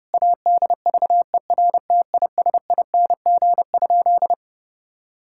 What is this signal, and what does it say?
Morse code, heard as a keyed tone, about 28 wpm, 700 Hz. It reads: ADVERTISING?
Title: Sentences from Top 1000 Words - Review Part 6 - 30wpm